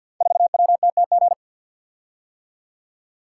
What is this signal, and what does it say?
Morse code, heard as a keyed tone, about 50 wpm, 700 Hz. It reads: 4WTTG
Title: Code Groups - 5 Letter:Numbers - 50wpm